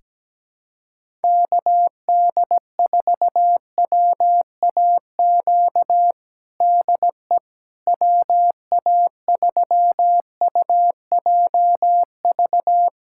Morse code, heard as a keyed tone, about 17 wpm, 700 Hz. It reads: KD4WAQ DE WA3UJV
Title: USA Callsign DE Callsign 17wpm